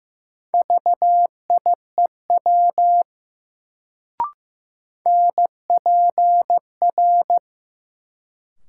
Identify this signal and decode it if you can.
Morse code, heard as a keyed tone, about 15 wpm, 700 Hz. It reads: VIEW NPR
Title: Sets of 5 Words - New Word Drill - 15wpm